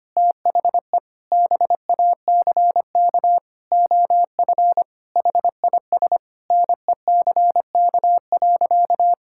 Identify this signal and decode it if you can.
Morse code, heard as a keyed tone, about 25 wpm, 700 Hz. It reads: THE BACK OF HIS NECK.
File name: practice / 25wpm / Sentences with 4-Word Phrases - 25wpm